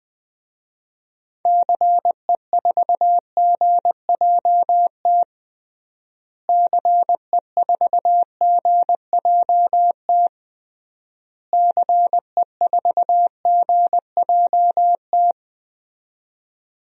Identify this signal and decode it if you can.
Morse code, heard as a keyed tone, about 20 wpm, 700 Hz. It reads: CE4GJT CE4GJT CE4GJT
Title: DXCC-100 Call Signs - 6 Characters - Repeated 3x - 20wpm